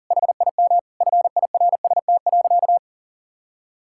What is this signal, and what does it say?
Morse code, heard as a keyed tone, about 40 wpm, 700 Hz. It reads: HIM FIRST.
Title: Sentences from Top 1000 Words - Review Part 6 - 40wpm